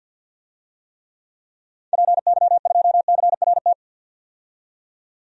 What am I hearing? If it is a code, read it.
Morse code, heard as a keyed tone, about 50 wpm, 700 Hz. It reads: PY2CRT
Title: DXCC-100 Call Signs - 6 Characters - 50wpm